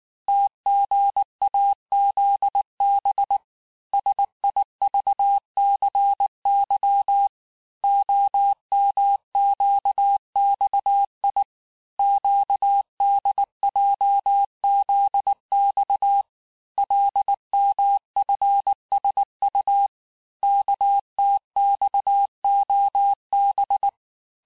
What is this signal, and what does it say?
Morse code, heard as a keyed tone, about 19 wpm, 800 Hz. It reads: TGAZB SIVCY OMQXI QDJZX LMFSU KTXOB
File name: cours-du-mercredi / audios / cours-869-texte-CHIFFRE-vitesse-18-MPM